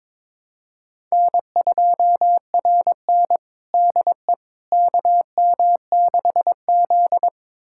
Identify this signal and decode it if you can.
Morse code, heard as a keyed tone, about 21 wpm, 700 Hz. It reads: N2RN DE KM6Z
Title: USA Callsign DE Callsign - 4 Characters - 22wpm